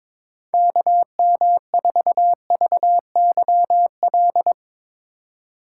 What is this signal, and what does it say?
Morse code, heard as a keyed tone, about 22 wpm, 700 Hz. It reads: KM4VYL
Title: USA Call Signs - 6 Characters - 2x Character Spacing - 22wpm